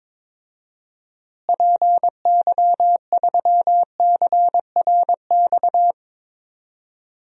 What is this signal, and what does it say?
Morse code, heard as a keyed tone, about 23 wpm, 700 Hz. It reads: PY3CRX
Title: DXCC-100 Call Signs - 6 Characters - Repeated 3x - 22wpm